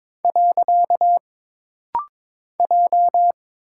Morse code, heard as a keyed tone, about 22 wpm, 700 Hz. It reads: . J
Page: This rapid-fire timing is geared to push your ICR (Instant Character Recognition) skills.